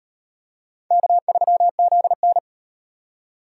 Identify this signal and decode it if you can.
Morse code, heard as a keyed tone, about 36 wpm, 700 Hz. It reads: K3ZN